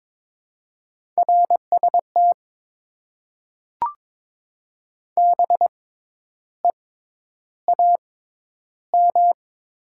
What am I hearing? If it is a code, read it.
Morse code, heard as a keyed tone, about 22 wpm, spaced as for 10 wpm, 700 Hz. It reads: RST BEAM
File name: Top 90 QSO Elements - 6x Character Spacing - 22wpm